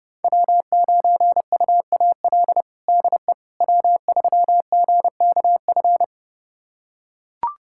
Morse code, heard as a keyed tone, about 30 wpm, 700 Hz. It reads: W9UAL DE W3GKF
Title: USA Callsign DE Callsign - 5 Characters - 30wpm